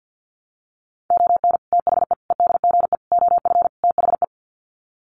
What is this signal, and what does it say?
Morse code, heard as a keyed tone, about 50 wpm, 700 Hz. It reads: ON THE EDGE OF THE
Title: Sets of 5 Words 50wpm